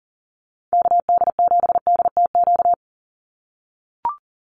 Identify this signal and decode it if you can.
Morse code, heard as a keyed tone, about 40 wpm, 700 Hz. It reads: KD7DTQ
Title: USA Call Signs 40wpm